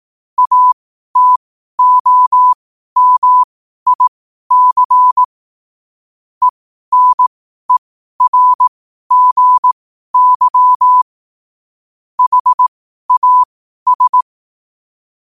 Morse code, heard as a keyed tone, about 18 wpm, spaced as for 12 wpm, 1000 Hz. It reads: ATOMIC ENERGY HAS
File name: qotdfiles / 12wpm